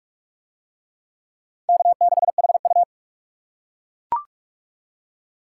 Morse code, heard as a keyed tone, about 39 wpm, 700 Hz. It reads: K6HU